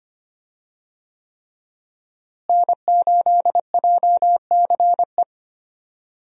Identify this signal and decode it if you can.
Morse code, heard as a keyed tone, about 25 wpm, 700 Hz. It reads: N8JCE
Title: USA Call Signs - 5 Characters - Repeated 3x - 25wpm